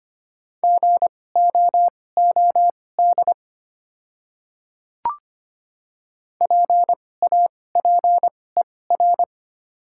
Morse code, heard as a keyed tone, about 25 wpm, spaced as for 18 wpm, 690 Hz. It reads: GOOD PAPER
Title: 2x Character Spacing - Top 300 Words - 25wpm